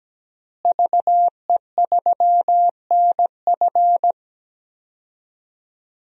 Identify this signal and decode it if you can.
Morse code, heard as a keyed tone, about 17 wpm, 700 Hz. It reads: VE3NF
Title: DXCC-100 Call Signs - 5 Characters - Repeated 2x - 17wpm